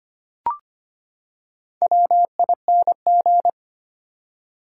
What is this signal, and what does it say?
Morse code, heard as a keyed tone, about 25 wpm, 700 Hz. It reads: WING
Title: Four Letter Words - 25wpm